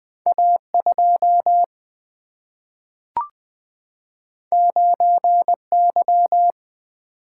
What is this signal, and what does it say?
Morse code, heard as a keyed tone, about 20 wpm, 700 Hz. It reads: A 2 9 Y
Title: Single-Letter Single-Number 20wpm